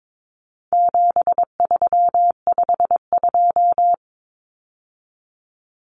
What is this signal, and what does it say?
Morse code, heard as a keyed tone, about 22 wpm, 700 Hz. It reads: 7352
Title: Numbers - 4-Digit 22wpm